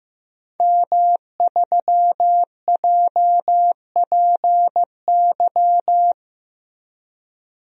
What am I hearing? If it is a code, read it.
Morse code, heard as a keyed tone, about 15 wpm, 700 Hz. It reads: M3JPY
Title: DXCC-100 Call Signs - 5 Characters - Repeated 3x - 15wpm